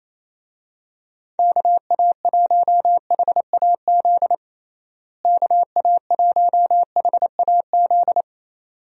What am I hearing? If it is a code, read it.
Morse code, heard as a keyed tone, about 28 wpm, 700 Hz. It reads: KA1HAZ KA1HAZ
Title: USA Call Signs - 6 Characters - Repeated 2x - 28wpm